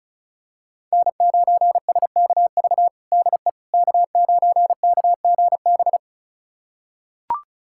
Morse code, heard as a keyed tone, about 36 wpm, 700 Hz. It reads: N9SKV DE K9KGB
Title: USA Callsign DE Callsign - 5 Characters - 35wpm